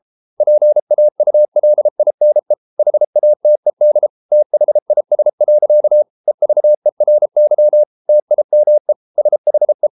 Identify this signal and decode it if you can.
Morse code, heard as a keyed tone, about 33 wpm, 600 Hz. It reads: PAULINE HATED THIS. EVERY TIME SHE